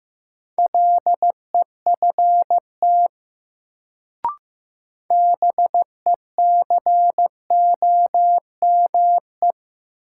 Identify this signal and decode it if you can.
Morse code, heard as a keyed tone, about 15 wpm, 700 Hz. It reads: LEFT BECOME
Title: Sets of 5 Words Encore - New Word Drill - 15wpm